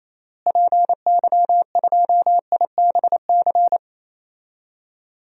Morse code, heard as a keyed tone, about 28 wpm, 700 Hz. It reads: PY2IBC
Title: DXCC-100 Call Signs - 6 Characters - Repeated 3x - 28wpm